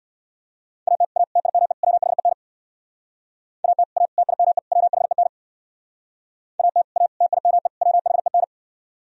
Morse code, heard as a keyed tone, about 66 wpm, 700 Hz. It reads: AT A TIME WHEN AT A TIME WHEN AT A TIME WHEN